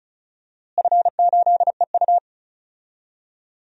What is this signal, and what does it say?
Morse code, heard as a keyed tone, about 34 wpm, 700 Hz. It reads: F8EU
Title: DXCC-100 Call Signs - 4 Characters - Repeated 2x - 35wpm